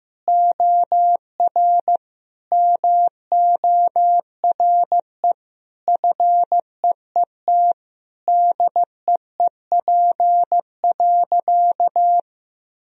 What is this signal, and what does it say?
Morse code, heard as a keyed tone, about 15 wpm, 700 Hz. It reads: OR MORE FEET DEEP.
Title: Sentences from Top 1000 Words - Review Part 3 - 15wpm